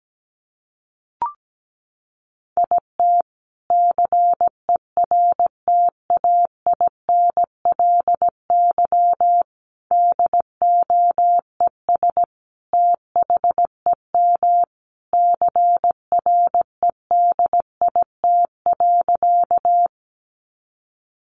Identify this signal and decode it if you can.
Morse code, heard as a keyed tone, about 17 wpm, 700 Hz. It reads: IT CERTAINLY DOES THEM CREDIT.
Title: Sentences from Top 1000 Words 17wpm